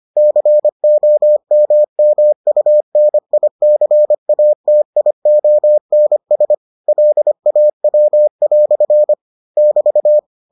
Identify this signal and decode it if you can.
Morse code, heard as a keyed tone, about 25 wpm, 600 Hz. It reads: COMMUNICATIONS LAW" <BT>